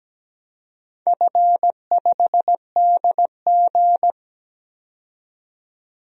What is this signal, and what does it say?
Morse code, heard as a keyed tone, about 17 wpm, 700 Hz. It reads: F5DG
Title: DXCC-100 Call Signs - 4 Characters - 17wpm